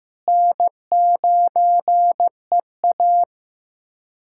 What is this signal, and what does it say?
Morse code, heard as a keyed tone, about 16 wpm, 700 Hz. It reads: N9EA
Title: USA Callsign DE Callsign - 4 Characters - 15wpm